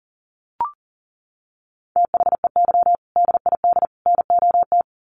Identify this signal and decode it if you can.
Morse code, heard as a keyed tone, about 40 wpm, 700 Hz. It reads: THEY DID NOT
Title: Sets of 3 Words 40wpm